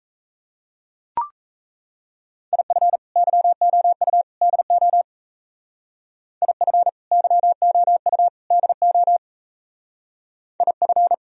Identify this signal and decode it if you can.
Morse code, heard as a keyed tone, about 34 wpm, 700 Hz. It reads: IF YOU DO IF YOU DO IF
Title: Speed-Racing - Sets of 3 Words 28wpm